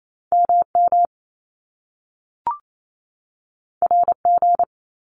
Morse code, heard as a keyed tone, about 28 wpm, 700 Hz. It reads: MM RG
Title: Top Two Letter Combinations in Words 28wpm